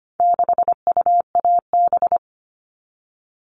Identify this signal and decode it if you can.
Morse code, heard as a keyed tone, about 25 wpm, 700 Hz. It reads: 6UAB
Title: All Cut Numbers - 4-Digit 25wpm